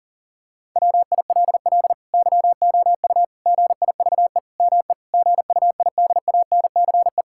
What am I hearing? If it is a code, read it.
Morse code, heard as a keyed tone, about 40 wpm, 700 Hz. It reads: WILL YOU GIVE ME GUIDANCE
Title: Sentences from Top 1600 Words 40wpm